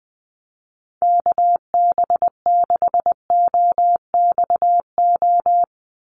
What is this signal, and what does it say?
Morse code, heard as a keyed tone, about 20 wpm, 700 Hz. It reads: KB6OXO
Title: USA Call Signs 20wpm